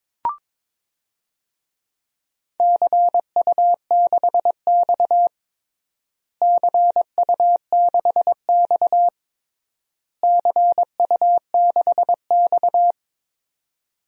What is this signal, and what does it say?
Morse code, heard as a keyed tone, about 21 wpm, 700 Hz. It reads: CU6X CU6X CU6X
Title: DXCC-100 Call Signs - 4 Characters - Repeated 3x - 22wpm